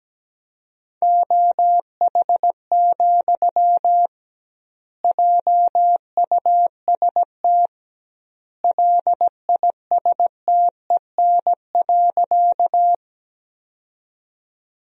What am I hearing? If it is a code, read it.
Morse code, heard as a keyed tone, about 17 wpm, 700 Hz. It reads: OH, JUST LISTEN.
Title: 2x Word-Spacing - Sentences from Top 500 Words - 17wpm